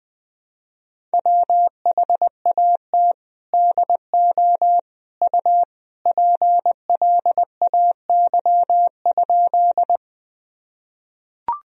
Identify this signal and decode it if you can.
Morse code, heard as a keyed tone, about 20 wpm, 700 Hz. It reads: WHAT DO U PLAY?
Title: Sentences from Top 200 Words and Abbreviations 20wpm